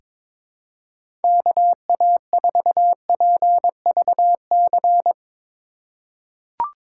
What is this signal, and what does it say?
Morse code, heard as a keyed tone, about 22 wpm, 700 Hz. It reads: KA4PVC
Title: USA Call Signs - 6 Characters - 3x Character Spacing - 22wpm